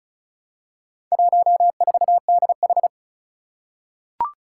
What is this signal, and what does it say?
Morse code, heard as a keyed tone, about 36 wpm, 700 Hz. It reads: 14DH
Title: Four Challenging Letter-Number Combinations 35wpm